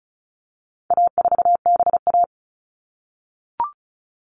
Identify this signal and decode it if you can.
Morse code, heard as a keyed tone, about 34 wpm, 700 Hz. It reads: A4BA
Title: All Cut Numbers - 4-Digit 35wpm